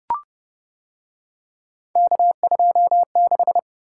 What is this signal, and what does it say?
Morse code, heard as a keyed tone, about 29 wpm, 700 Hz. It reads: K26